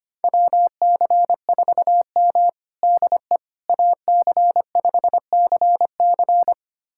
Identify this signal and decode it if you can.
Morse code, heard as a keyed tone, about 25 wpm, 700 Hz. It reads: WC4M DE AC5CC
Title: USA Callsign DE Callsign 25wpm